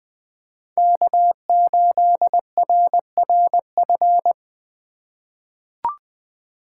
Morse code, heard as a keyed tone, about 20 wpm, 700 Hz. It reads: K8RRF
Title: USA Call Signs - 5 Characters - 2x Character Spacing - 20wpm